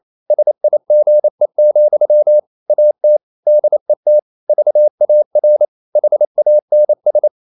Morse code, heard as a keyed tone, about 28 wpm, 600 Hz. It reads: SIGE, AT DET VAR HANS
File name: Ferie QRQ 28wpm